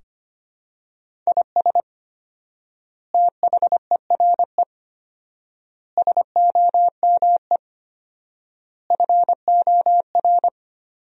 Morse code, heard as a keyed tone, about 25 wpm, 700 Hz. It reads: IS THERE SOME FOR
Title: 4x Word-Spacing - Sentences from Top 200 Words - 25wpm